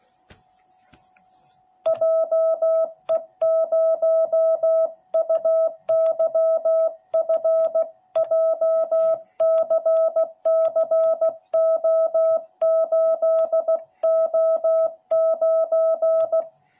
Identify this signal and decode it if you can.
Morse code, heard as a keyed tone, about 16 wpm, 660 Hz. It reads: JE0UYF JCCO8O9